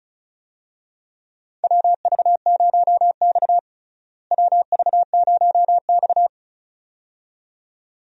Morse code, heard as a keyed tone, about 36 wpm, 700 Hz. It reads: WV0X WV0X
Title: USA Call Signs - 4 Characters - Repeated 2x - 35wpm